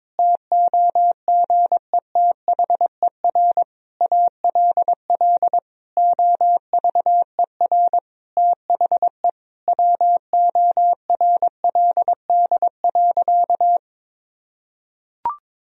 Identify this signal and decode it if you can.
Morse code, heard as a keyed tone, about 22 wpm, 700 Hz. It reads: TOGETHER ALL OVER THE WORLD.
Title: Sentences from Top 1000 Words - Review Part 2 - 22wpm